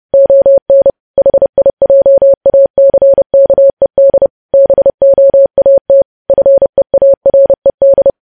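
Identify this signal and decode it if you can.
Morse code, heard as a keyed tone, about 30 wpm, 550 Hz. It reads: ON HIJACKED BOAT FEARED
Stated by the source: News headlines sent in Morse at 30 WPM.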